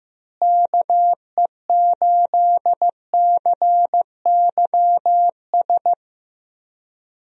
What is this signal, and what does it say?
Morse code, heard as a keyed tone, about 15 wpm, 700 Hz. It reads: KE8CYS
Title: USA Call Signs - 6 Characters - Repeated 2x - 15wpm